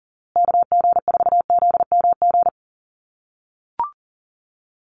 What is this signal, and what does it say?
Morse code, heard as a keyed tone, about 39 wpm, 700 Hz. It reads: KG4ZMG
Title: USA Call Signs Repeated 2x 40wpm